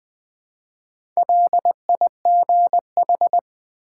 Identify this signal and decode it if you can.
Morse code, heard as a keyed tone, about 20 wpm, 700 Hz. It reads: LIGH
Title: Build A Word - 1 Letter At A Time - 3 to 5-Letter Words - 20wpm